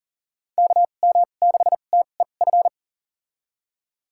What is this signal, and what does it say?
Morse code, heard as a keyed tone, about 40 wpm, spaced as for 29 wpm, 700 Hz. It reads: KM6TEF
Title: USA Call Signs - 6 Characters - 2x Character Spacing - 40wpm